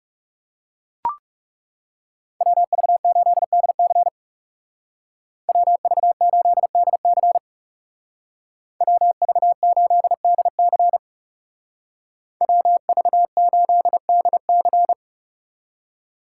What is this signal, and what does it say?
Morse code, heard as a keyed tone, about 35 wpm, 700 Hz. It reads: WV8DC WV8DC WV8DC WV8DC
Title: Speed Racing - USA 2x2 Call Signs 30wpm